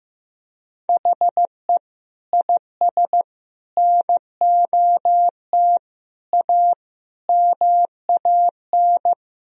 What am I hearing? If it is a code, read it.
Morse code, heard as a keyed tone, about 15 wpm, 700 Hz. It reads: HE IS NOT A MAN
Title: Sentences from Top 1000 Words - Review Part 4 - 15wpm